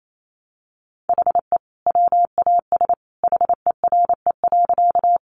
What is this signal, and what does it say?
Morse code, heard as a keyed tone, about 28 wpm, 700 Hz. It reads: HE WAS HERE.
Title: Sentences from Top 200 Words 28wpm